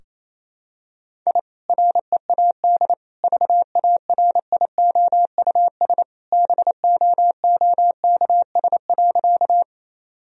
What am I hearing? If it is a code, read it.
Morse code, heard as a keyed tone, about 28 wpm, 700 Hz. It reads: I READ VARIOUS BOOKS.
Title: Sentences from Top 1000 Words - Review Part 4 - 28wpm